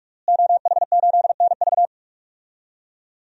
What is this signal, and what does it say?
Morse code, heard as a keyed tone, about 47 wpm, 700 Hz. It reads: OH8NV
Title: DXCC-100 Call Signs - 5 Characters - Repeated 3x - 45wpm